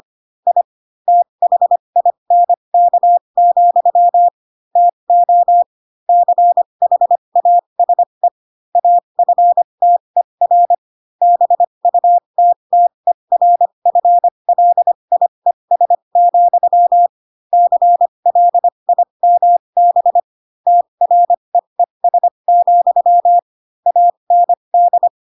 Morse code, heard as a keyed tone, about 25 wpm, spaced as for 22 wpm, 700 Hz. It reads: I THINK, TO CHASE AFTER BUTTERFLIES, CLIMB TREES, AND
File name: Pinocchio / Pinocchio ch4 18 25 ik1hge